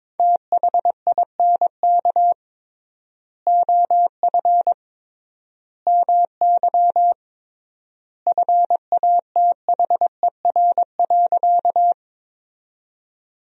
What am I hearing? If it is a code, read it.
Morse code, heard as a keyed tone, about 22 wpm, 700 Hz. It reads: THINK OF MY FATHER.
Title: 3x Word-Spacing - Sentences from Top 300 Words - 22wpm